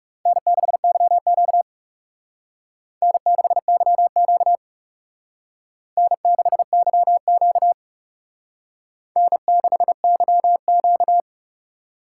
Morse code, heard as a keyed tone, about 38 wpm, 700 Hz. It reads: N6YQ N6YQ N6YQ N6YQ